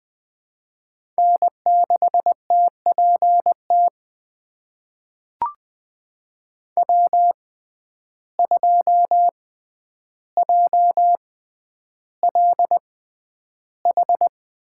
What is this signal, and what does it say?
Morse code, heard as a keyed tone, about 20 wpm, spaced as for 10 wpm, 700 Hz. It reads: N6TPT W2JLH
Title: USA Call Signs - 5 Characters - 6x Character Spacing - 20wpm